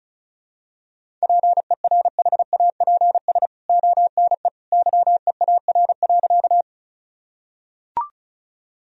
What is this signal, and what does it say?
Morse code, heard as a keyed tone, about 34 wpm, 700 Hz. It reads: PERHAPS ONE YEAR.
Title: Sentences from Top 1000 Words - Review Part 1 - 35wpm